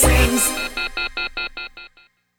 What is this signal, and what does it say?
Off-hook warning tone, 1400 + 2060 + 2450 + 2600 Hz, with noise.